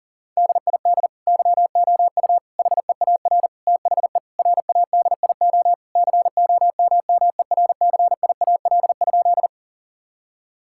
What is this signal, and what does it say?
Morse code, heard as a keyed tone, about 40 wpm, 700 Hz. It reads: DID YOU HEAR THE RADIO COMMERCIAL?
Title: Sentences from Top 1000 Words - Review Part 5 - 40wpm